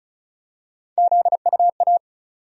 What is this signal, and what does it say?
Morse code, heard as a keyed tone, about 34 wpm, 700 Hz. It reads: ZUA
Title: USA Call Sign Postfixes - 3 Letters 35wpm